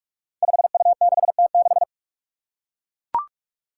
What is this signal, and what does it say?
Morse code, heard as a keyed tone, about 41 wpm, 700 Hz. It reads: 5U6T6
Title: Five Challenging Letter-Number Combinations 45wpm